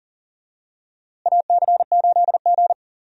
Morse code, heard as a keyed tone, about 40 wpm, 700 Hz. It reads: AC8G
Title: USA Call Signs - 4 Characters - Repeated 2x - 40wpm